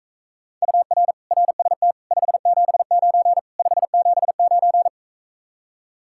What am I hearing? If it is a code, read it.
Morse code, heard as a keyed tone, about 44 wpm, 700 Hz. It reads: UR RST 579 579